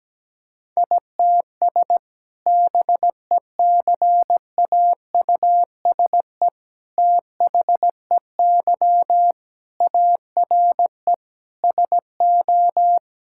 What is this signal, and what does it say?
Morse code, heard as a keyed tone, about 17 wpm, 700 Hz. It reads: ITS BECAUSE THEY ARE SO
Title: Sentences from Top 1000 Words - Review Part 6 - 17wpm